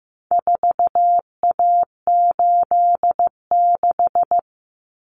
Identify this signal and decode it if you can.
Morse code, heard as a keyed tone, about 15 wpm, 700 Hz. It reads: 4A86
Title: Common Cut Numbers - 4-Digit 15wpm